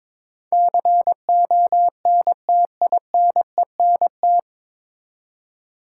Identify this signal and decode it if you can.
Morse code, heard as a keyed tone, about 22 wpm, 700 Hz. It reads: CONTINENT
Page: Morse Code Ninja practice 22wpm